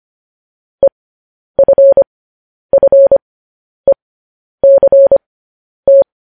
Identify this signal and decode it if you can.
Morse code, heard as a keyed tone, about 25 wpm, spaced as for 10 wpm, 550 Hz. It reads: EFFECT
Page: news-headlines-in-morse-code-at-10-wpm